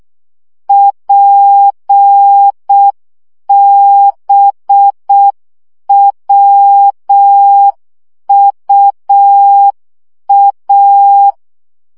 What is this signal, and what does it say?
Morse code, heard as a keyed tone, about 6 wpm, 800 Hz. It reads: PBWUA
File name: morze vizsga 2011-3 5